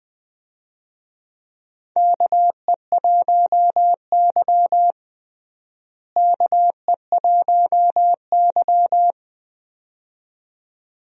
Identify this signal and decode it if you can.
Morse code, heard as a keyed tone, about 20 wpm, 700 Hz. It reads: KE1Y KE1Y
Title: USA Call Signs - 4 Characters - Repeated 2x - 20wpm